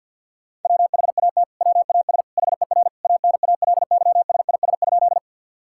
Morse code, heard as a keyed tone, about 47 wpm, 700 Hz. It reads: WHAT WAS HER ANALYSIS?
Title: Sentences from Top 1000 Words - Review Part 5 - 50wpm